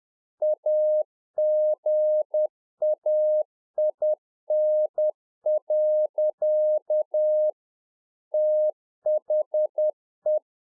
Morse code, heard as a keyed tone, about 10 wpm, 610 Hz. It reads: AGAIN. THE
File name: 2011-04-mp3 / RICK JJ0MPI 10WPM